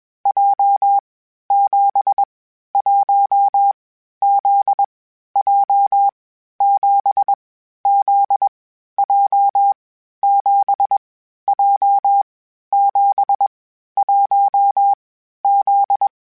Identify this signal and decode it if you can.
Morse code, heard as a keyed tone, about 21 wpm, 800 Hz. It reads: J 7 1 Z J 7 Z J 7 J 7 1 Z
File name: Basic Sets / Basic Letter Groups Collective7 / JZ17@20